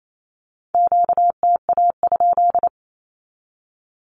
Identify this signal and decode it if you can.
Morse code, heard as a keyed tone, about 28 wpm, 700 Hz. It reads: QTA?
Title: Q-Codes 28wpm